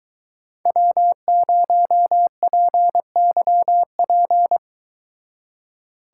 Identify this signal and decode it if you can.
Morse code, heard as a keyed tone, about 22 wpm, 700 Hz. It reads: W0PYP